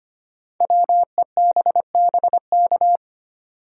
Morse code, heard as a keyed tone, about 25 wpm, 700 Hz. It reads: WEBBK
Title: Code Groups - 5 Letter:Numbers - 25wpm